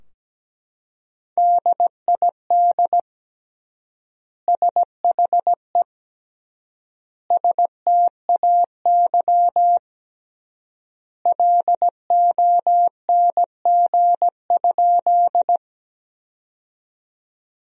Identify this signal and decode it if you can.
Morse code, heard as a keyed tone, about 17 wpm, 700 Hz. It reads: DID SHE STAY LONG?
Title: 3x Word-Spacing - Sentences from Top 500 Words - 17wpm